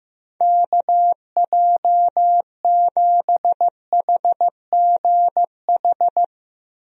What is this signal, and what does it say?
Morse code, heard as a keyed tone, about 15 wpm, 700 Hz. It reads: KJ7HGH
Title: USA Call Signs - 6 Characters - Repeated 2x - 15wpm